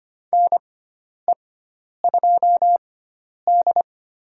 Morse code, heard as a keyed tone, about 25 wpm, spaced as for 10 wpm, 700 Hz. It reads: NE2D
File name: Lesson 16g - Letter c - Call Signs 25wpm